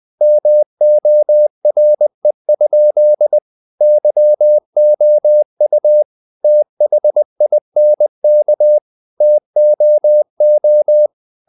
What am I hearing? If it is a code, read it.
Morse code, heard as a keyed tone, about 20 wpm, 600 Hz. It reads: MORE? YOU THINK TOO
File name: the old man and the sea part 2 20wpm